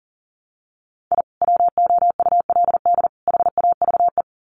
Morse code, heard as a keyed tone, about 40 wpm, 700 Hz. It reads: I WOULD HAVE
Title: Sets of 3 Words 40wpm